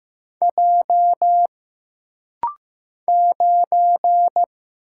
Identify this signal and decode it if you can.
Morse code, heard as a keyed tone, about 15 wpm, 700 Hz. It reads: J 9
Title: Single Letter-Number Alt - Rapid-Fire 15wpm